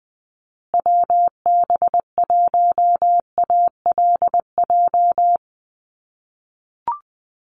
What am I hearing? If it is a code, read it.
Morse code, heard as a keyed tone, about 20 wpm, 700 Hz. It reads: WB1ALJ
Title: USA Call Signs Repeated 2x 20wpm